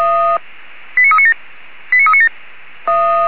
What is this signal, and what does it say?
Five-tone selective call (ZVEI), heard as quick runs of five tones, tones of about 70 ms. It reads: 89397 89397